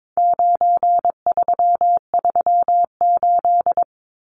Morse code, heard as a keyed tone, about 22 wpm, 700 Hz. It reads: 9338
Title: Numbers - 4-Digit 22wpm